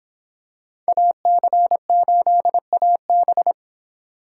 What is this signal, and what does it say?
Morse code, heard as a keyed tone, about 25 wpm, 700 Hz. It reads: AC8AB